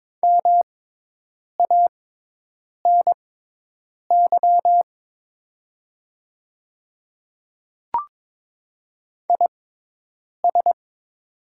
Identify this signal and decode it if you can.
Morse code, heard as a keyed tone, about 23 wpm, spaced as for 8 wpm, 700 Hz. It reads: MANY IS
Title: 6x Character Spacing - Top 100 Words - 22wpm